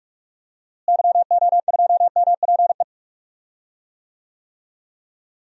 Morse code, heard as a keyed tone, about 46 wpm, 700 Hz. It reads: YO2MPE
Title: DXCC-100 Call Signs - 6 Characters - Repeated 2x - 45wpm